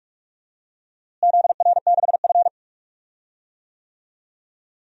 Morse code, heard as a keyed tone, about 46 wpm, 700 Hz. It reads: ZR6F
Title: DXCC-100 Call Signs - 4 Characters - Repeated 2x - 45wpm